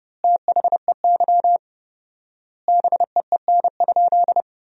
Morse code, heard as a keyed tone, about 30 wpm, 700 Hz. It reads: THEY BEEN?
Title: 4x Word-Spacing - Sentences from Top 200 Words - 30wpm